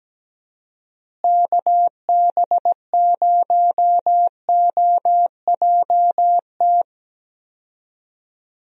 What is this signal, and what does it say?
Morse code, heard as a keyed tone, about 17 wpm, 700 Hz. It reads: KB0OJT
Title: USA Call Signs - 6 Characters - Repeated 2x - 17wpm